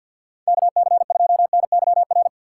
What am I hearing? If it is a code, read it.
Morse code, heard as a keyed tone, about 50 wpm, 700 Hz. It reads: KC2NYR
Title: USA Call Signs - 6 Characters - 3x Character Spacing - 50wpm